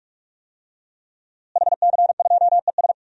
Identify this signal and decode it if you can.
Morse code, heard as a keyed tone, about 45 wpm, 700 Hz. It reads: HC2ES
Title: DXCC-100 Call Signs - 5 Characters - Repeated 3x - 45wpm